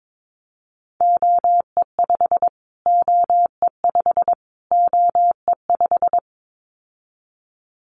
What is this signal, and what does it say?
Morse code, heard as a keyed tone, about 22 wpm, 700 Hz. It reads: OE5 OE5 OE5
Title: Callsign Prefixes for Top 25 DXCC - Repeated 3x - 22wpm